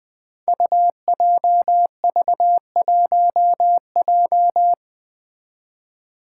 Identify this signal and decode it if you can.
Morse code, heard as a keyed tone, about 20 wpm, 700 Hz. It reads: UJV1J
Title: Five Challenging Letter-Number Combinations 20wpm